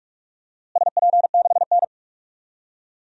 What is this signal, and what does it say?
Morse code, heard as a keyed tone, about 42 wpm, 700 Hz. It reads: SP6N
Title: DXCC-100 Call Signs - 4 Characters - Repeated 2x - 45wpm